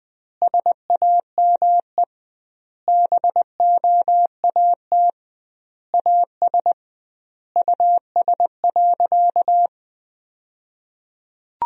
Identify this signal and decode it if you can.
Morse code, heard as a keyed tone, about 20 wpm, 700 Hz. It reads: SAME BOAT AS US.
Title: 2x Word-Spacing - Sentences from Top 500 Words - 20wpm